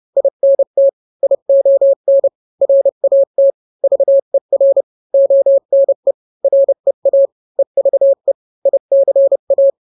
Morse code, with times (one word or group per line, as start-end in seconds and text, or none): 0.2-0.9 INT
1.2-2.3 ION
2.6-3.5 RAT
3.8-4.8 VER
5.1-6.1 ONE
6.4-7.3 REA
7.6-8.3 EVE
8.7-9.7 ICA